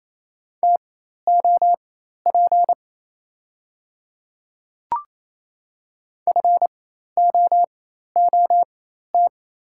Morse code, heard as a keyed tone, about 28 wpm, spaced as for 13 wpm, 690 Hz. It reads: TOP FOOT
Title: 4x Character Spacing - Top 500 Words - 28wpm